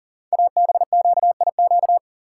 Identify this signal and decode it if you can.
Morse code, heard as a keyed tone, about 43 wpm, 700 Hz. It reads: ABQIQ
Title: Code Groups - 5 Letters - 40wpm